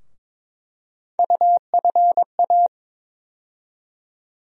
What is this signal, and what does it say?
Morse code, heard as a keyed tone, about 22 wpm, 700 Hz. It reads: UFA
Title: USA Call Sign Postfixes - 3 Letters 22wpm